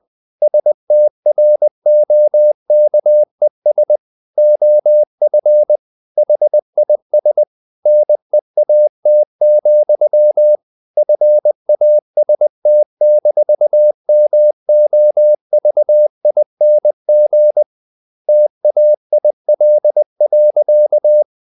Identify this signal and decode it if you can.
Morse code, heard as a keyed tone, about 20 wpm, 600 Hz. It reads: STROKES OF HIS NEAT, FAST-MOVING TAIL.
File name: the old man and the sea / the old man and the sea part 1 20wpm